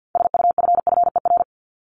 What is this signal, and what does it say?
Morse code, heard as a keyed tone, about 50 wpm, 700 Hz. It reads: SUFFER
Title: ING Suffix - Top 100 Words with Base Words 50wpm